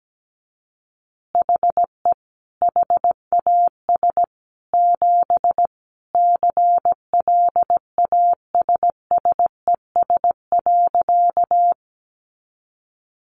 Morse code, heard as a keyed tone, about 17 wpm, 700 Hz. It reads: HE HAS 7 CLASSES.
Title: Additional Easy Phrases 17wpm